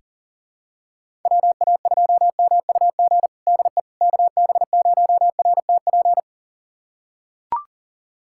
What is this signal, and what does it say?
Morse code, heard as a keyed tone, about 40 wpm, 700 Hz. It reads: WA2MUG DE KB0RTP
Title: USA Callsign DE Callsign - 6 Characters - 40wpm